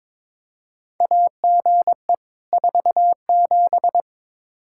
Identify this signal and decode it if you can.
Morse code, heard as a keyed tone, about 22 wpm, 700 Hz. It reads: AGE 47
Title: Practice QSO - Exchange 3 - More Info - 22wpm